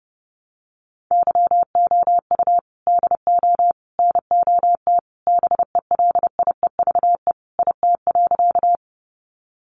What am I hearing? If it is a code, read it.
Morse code, heard as a keyed tone, about 30 wpm, 700 Hz. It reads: YOU DO NOT BELIEVE IT.
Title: Sentences from Top 1000 Words 30wpm